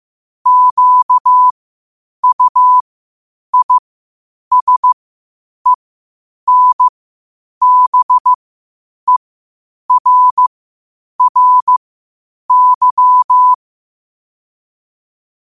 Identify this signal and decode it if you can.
Morse code, heard as a keyed tone, about 15 wpm, spaced as for 8 wpm, 1000 Hz. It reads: QUISENBERRY
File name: qotdfiles / 8wpm